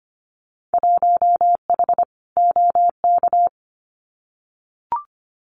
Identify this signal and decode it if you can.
Morse code, heard as a keyed tone, about 24 wpm, 700 Hz. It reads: 1H OK
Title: Winter Field Day Exchanges 25wpm